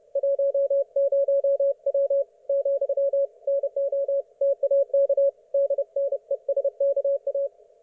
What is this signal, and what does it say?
Morse code, heard as a keyed tone, about 31 wpm, 560 Hz. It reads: 10W , NOTAK DNESKA
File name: Archiv / CUC179 / ok1fhi cul179